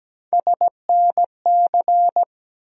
Morse code, heard as a keyed tone, about 17 wpm, 700 Hz. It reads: SNC
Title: USA Call Sign Postfixes - 3 Letters 17wpm